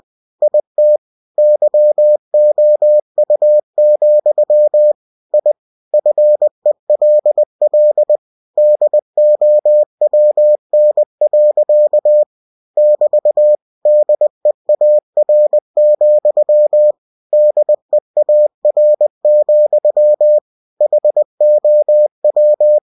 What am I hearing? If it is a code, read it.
Morse code, heard as a keyed tone, about 20 wpm, 600 Hz. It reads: IT YOU, I FELL DOWN. <BT> DEAR, DEAR, HOW
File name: cw-books / winnie-the-pooh / winnie-20wpm-page-0055